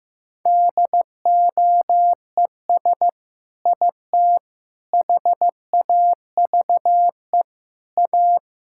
Morse code, heard as a keyed tone, about 15 wpm, 700 Hz. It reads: DOES IT HAVE A
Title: Sentences from Top 1000 Words - Review Part 5 - 15wpm